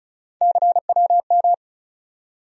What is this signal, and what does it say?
Morse code, heard as a keyed tone, about 36 wpm, 700 Hz. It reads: CWM
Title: USA Call Sign Postfixes - 3 Letters 35wpm